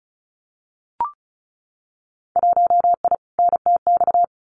Morse code, heard as a keyed tone, about 36 wpm, 700 Hz. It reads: 1I NTX
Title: Winter Field Day Exchanges 35wpm